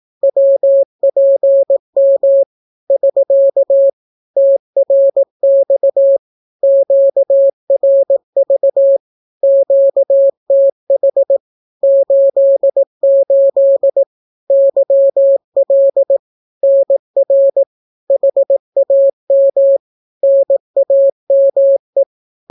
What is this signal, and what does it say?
Morse code, heard as a keyed tone, about 18 wpm, 550 Hz. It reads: WPM <SK> TRX QRV QTH 88 YL NR HAM NAME
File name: naukapodstaw / skroty / skroty